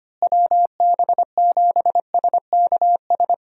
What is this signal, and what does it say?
Morse code, heard as a keyed tone, about 25 wpm, 700 Hz. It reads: WB7SKS
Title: USA Call Signs - 6 Characters - Repeated 3x - 25wpm